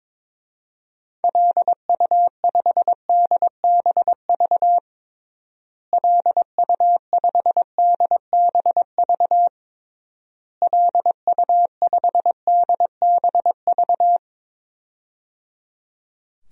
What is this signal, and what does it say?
Morse code, heard as a keyed tone, about 22 wpm, 700 Hz. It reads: LU5DBV LU5DBV LU5DBV
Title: DXCC-100 Call Signs - 6 Characters - Repeated 3x - 22wpm